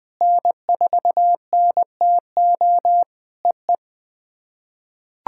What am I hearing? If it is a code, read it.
Morse code, heard as a keyed tone, about 20 wpm, 700 Hz. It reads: N4NTO EE
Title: POTA QSO Practice - 20wpm